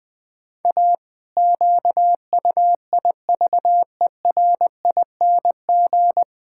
Morse code, heard as a keyed tone, about 20 wpm, 700 Hz. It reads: A QUIVERING
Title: Pangram Challenge v2 - 20wpm